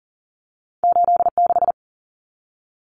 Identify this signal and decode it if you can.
Morse code, heard as a keyed tone, about 40 wpm, 700 Hz. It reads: 86
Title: Common Cut Numbers - 2-Digit 40wpm